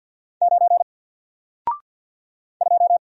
Morse code, heard as a keyed tone, about 50 wpm, 700 Hz. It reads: 9 2
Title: Single Letter-Number 50wpm